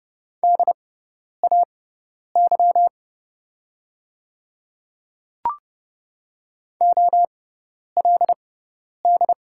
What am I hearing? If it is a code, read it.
Morse code, heard as a keyed tone, about 30 wpm, spaced as for 10 wpm, 700 Hz. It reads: DAY OLD
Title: 6x Character Spacing - Top 200 Words - 30wpm